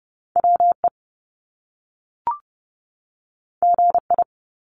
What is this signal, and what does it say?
Morse code, heard as a keyed tone, about 30 wpm, 700 Hz. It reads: WE GI
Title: Top Two Letter Combinations in Words 30wpm